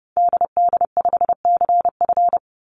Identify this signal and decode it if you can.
Morse code, heard as a keyed tone, about 30 wpm, 700 Hz. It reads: DD5CF
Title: Call Signs Repeated 3x - 30wpm